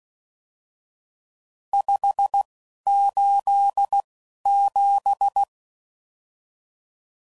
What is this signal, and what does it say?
Morse code, heard as a keyed tone, about 16 wpm, spaced as for 11 wpm, 800 Hz. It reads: 587
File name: CW Code Groups @10WPM / RSTs / Various RST No's